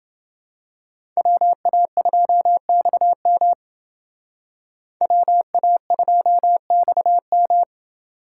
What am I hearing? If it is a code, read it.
Morse code, heard as a keyed tone, about 28 wpm, 700 Hz. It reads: WA2XM WA2XM